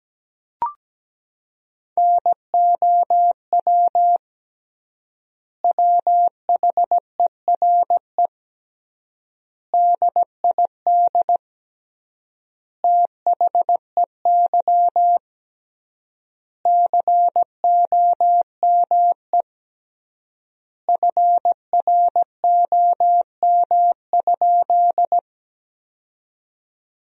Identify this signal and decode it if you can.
Morse code, heard as a keyed tone, about 17 wpm, 700 Hz. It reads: NOW WHERE DID THEY COME FROM?
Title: 3x Word-Spacing - Sentences from Top 200 Words - 17wpm